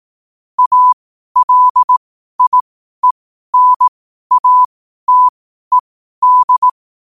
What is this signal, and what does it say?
Morse code, heard as a keyed tone, about 18 wpm, spaced as for 12 wpm, 1000 Hz. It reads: ALIENATED
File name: qotdfiles / 12wpm